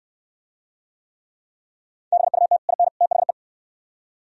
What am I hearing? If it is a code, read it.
Morse code, heard as a keyed tone, about 69 wpm, 700 Hz. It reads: BUT IN THE